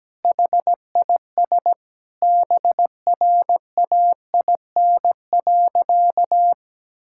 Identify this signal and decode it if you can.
Morse code, heard as a keyed tone, about 17 wpm, 700 Hz. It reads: HIS BRAIN.
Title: Sentences from Top 1600 Words 17wpm